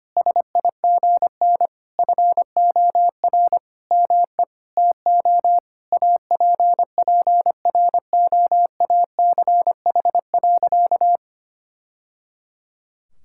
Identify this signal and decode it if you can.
Morse code, heard as a keyed tone, about 25 wpm, 700 Hz. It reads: SIGN FOR ME TO APPROACH.
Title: Sentences from Top 1000 Words - Review Part 6 - 25wpm